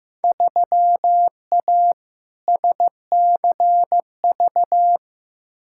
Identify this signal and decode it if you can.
Morse code, heard as a keyed tone, about 15 wpm, 700 Hz. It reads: 3A SCV
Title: Field Day Exchanges - 15wpm